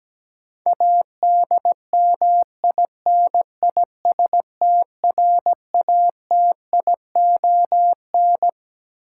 Morse code, heard as a keyed tone, about 17 wpm, 700 Hz. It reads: ADMINISTRATION
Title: Sets of 5 Words - New Word Drill - 17wpm